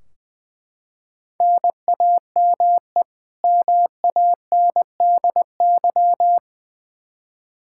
Morse code, heard as a keyed tone, about 21 wpm, 700 Hz. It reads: NAME MANDY
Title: Practice QSO - Exchange 1 - Beginning - 20wpm